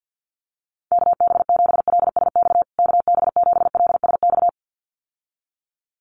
Morse code, heard as a keyed tone, about 50 wpm, 700 Hz. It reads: KB7LSX KB7LSX
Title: USA Call Signs Repeated 2x 50wpm